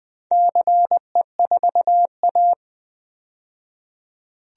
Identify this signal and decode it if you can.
Morse code, heard as a keyed tone, about 20 wpm, 700 Hz. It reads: CE4A
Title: DXCC-100 Call Signs - 4 Characters - 20wpm